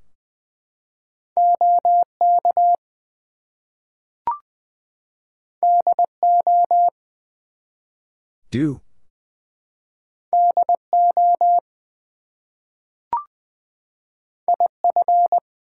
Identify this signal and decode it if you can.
Morse code, heard as a keyed tone, about 20 wpm, 700 Hz. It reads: OK DO DO IF
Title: Two Letter Words - 20wpm